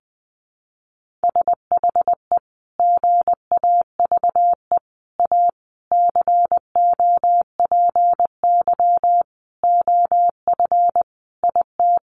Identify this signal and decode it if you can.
Morse code, heard as a keyed tone, about 20 wpm, 700 Hz. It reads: SHE GAVE A COPY OF IT
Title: Sentences from Top 1400 Words 20wpm